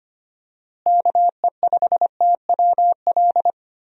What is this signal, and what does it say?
Morse code, heard as a keyed tone, about 25 wpm, 700 Hz. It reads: KE5TWL
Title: USA Call Signs - 6 Characters - Repeated 3x - 25wpm